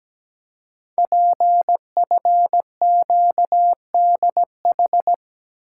Morse code, heard as a keyed tone, about 17 wpm, 700 Hz. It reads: PFQDH
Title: Code Groups - 5 Letters - 17wpm